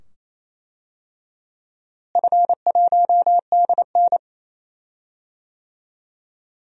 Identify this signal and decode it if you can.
Morse code, heard as a keyed tone, about 28 wpm, 700 Hz. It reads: F1DN
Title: DXCC-100 Call Signs - 4 Characters - Repeated 2x - 28wpm